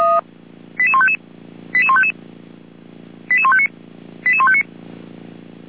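Five-tone selective call (ZVEI), heard as quick runs of five tones, tones of about 70 ms. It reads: 88155 88155 88159 88159
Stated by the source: ZVEI five-tone selective call 88419, received via SDR.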